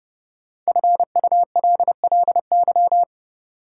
Morse code, heard as a keyed tone, about 30 wpm, 700 Hz. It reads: FULLY
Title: Sentences from Top 1300 Words - New Word Drill - 30wpm